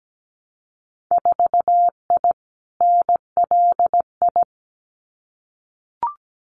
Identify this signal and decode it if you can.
Morse code, heard as a keyed tone, about 17 wpm, 700 Hz. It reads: 4I NLI
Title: Winter Field Day Exchanges Encore 17wpm